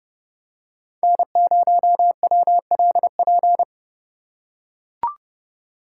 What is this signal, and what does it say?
Morse code, heard as a keyed tone, about 29 wpm, 700 Hz. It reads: N0WLP